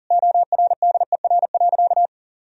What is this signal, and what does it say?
Morse code, heard as a keyed tone, about 37 wpm, 700 Hz. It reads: ORDER.
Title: Sentences from Top 1000 Words - Review Part 2 - 40wpm